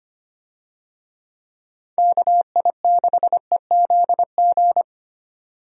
Morse code, heard as a keyed tone, about 25 wpm, 700 Hz. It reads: KI6EZG
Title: USA Call Signs - 6 Characters - 25wpm